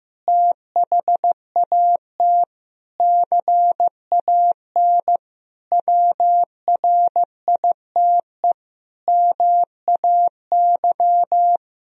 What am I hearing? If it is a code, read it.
Morse code, heard as a keyed tone, about 16 wpm, 700 Hz. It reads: THAT CAN WRITE MAY
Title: Sentences from Top 1000 Words - Review Part 1 - 15wpm